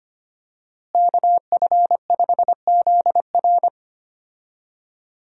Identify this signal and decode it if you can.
Morse code, heard as a keyed tone, about 25 wpm, 700 Hz. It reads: KF5ZR
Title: USA Call Signs - 5 Characters - Repeated 2x - 25wpm